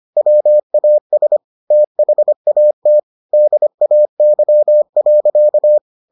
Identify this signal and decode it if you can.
Morse code, heard as a keyed tone, about 25 wpm, 600 Hz. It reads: WAS THAT DAY.